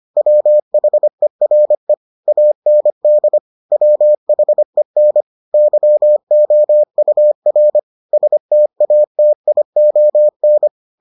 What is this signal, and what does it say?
Morse code, heard as a keyed tone, about 25 wpm, 600 Hz. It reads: WHERE AND WHEN YOUR STATION